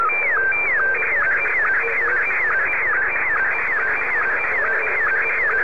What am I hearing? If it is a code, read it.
SSTV (slow-scan television), Scottie 1.